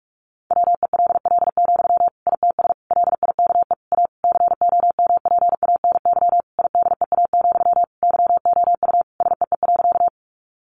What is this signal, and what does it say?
Morse code, heard as a keyed tone, about 44 wpm, 700 Hz. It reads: WELL, ITS LIKE A COMPANY IDEA, YOU SEE.
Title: Sentences from Top 600 Words 45wpm